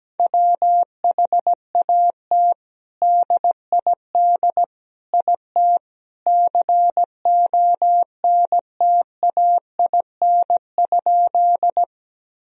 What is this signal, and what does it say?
Morse code, heard as a keyed tone, about 17 wpm, 700 Hz. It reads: WHAT DID IT CONTAIN?
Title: Sentences from Top 1000 Words - Review Part 2 - 17wpm